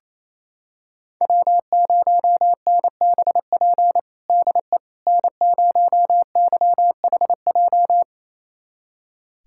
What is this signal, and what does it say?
Morse code, heard as a keyed tone, about 28 wpm, 700 Hz. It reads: W0NBP DE N0YHJ
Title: USA Callsign DE Callsign - 5 Characters - 28wpm